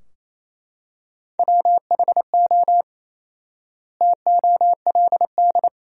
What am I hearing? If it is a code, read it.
Morse code, heard as a keyed tone, about 28 wpm, 700 Hz. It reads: WHO TOLD
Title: 4x Word-Spacing - Sentences from Top 400 Words - 28wpm